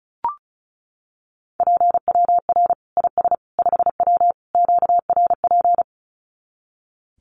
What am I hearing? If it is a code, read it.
Morse code, heard as a keyed tone, about 34 wpm, 700 Hz. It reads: PWR IS 5W QRP
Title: Common QSO Phrases 35wpm